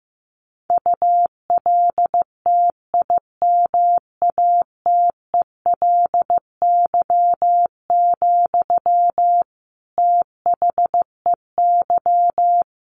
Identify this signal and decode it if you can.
Morse code, heard as a keyed tone, about 15 wpm, 700 Hz. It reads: ULTIMATELY, THEY
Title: Sentences from Top 1200 Words 15wpm